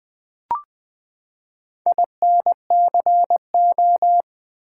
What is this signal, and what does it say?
Morse code, heard as a keyed tone, about 20 wpm, 700 Hz. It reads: INCO
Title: Build A Word - 4 Letters At A Time - 11-Letter Words - 20wpm